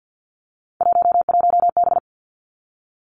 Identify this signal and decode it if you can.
Morse code, heard as a keyed tone, about 50 wpm, 700 Hz. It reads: 11B
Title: Three Challenging Letter-Number Combinations 50wpm